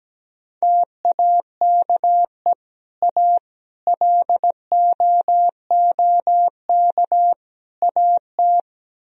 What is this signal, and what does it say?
Morse code, heard as a keyed tone, about 17 wpm, 700 Hz. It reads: TAKE A LOOK AT
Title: Speed-Racing - Sets of 4 Words 17wpm